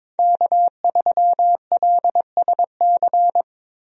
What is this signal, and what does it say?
Morse code, heard as a keyed tone, about 21 wpm, 700 Hz. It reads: K3LSC
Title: USA Call Signs - 5 Characters - Repeated 3x - 22wpm